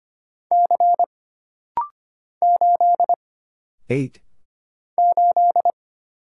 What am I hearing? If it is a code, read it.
Morse code, heard as a keyed tone, about 25 wpm, 700 Hz. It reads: C 8 8
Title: Single Letter-Number Alt - Rapid-Fire 25wpm